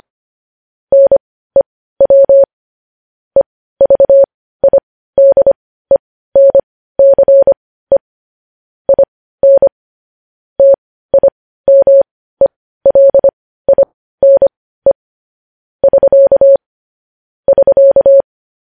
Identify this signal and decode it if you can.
Morse code, heard as a keyed tone, about 25 wpm, spaced as for 15 wpm, 550 Hz. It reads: NEW EVIDENCE IN TIMELINE <SK> <SK>
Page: news headlines in morse code at 15 wpm